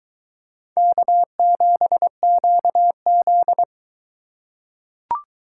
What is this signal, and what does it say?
Morse code, heard as a keyed tone, about 21 wpm, 700 Hz. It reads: K7QZ